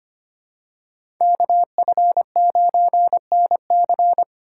Morse code, heard as a keyed tone, about 25 wpm, 700 Hz. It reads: KF9NC
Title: USA Call Signs - 5 Characters - 8x Character Spacing - 25wpm